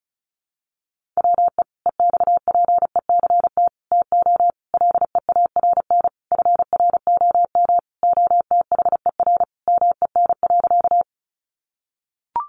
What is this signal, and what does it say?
Morse code, heard as a keyed tone, about 34 wpm, 700 Hz. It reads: WE EXPECT TO LEARN FROM OTHER MEN.
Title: Sentences from Top 1100 Words 35wpm